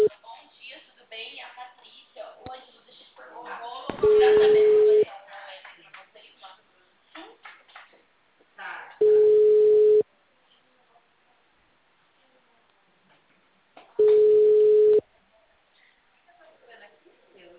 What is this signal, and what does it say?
Ringback tone (Europe, 425 Hz), with speech.